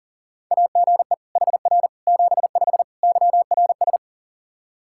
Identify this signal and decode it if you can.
Morse code, heard as a keyed tone, about 37 wpm, 700 Hz. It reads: AGE HR 75 YRS
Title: K1USN SST QSO Practice 40wpm